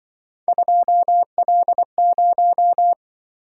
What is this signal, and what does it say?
Morse code, heard as a keyed tone, about 24 wpm, 700 Hz. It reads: 2L0
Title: Speed-Racing - Three Letter-Number Combinations - 20wpm